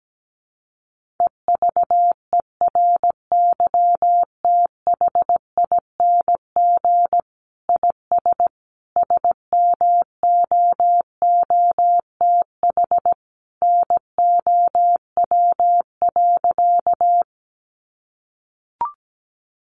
Morse code, heard as a keyed tone, about 17 wpm, 700 Hz. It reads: EVERYTHING IS SMOOTH NOW.
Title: Sentences from Top 1400 Words 17wpm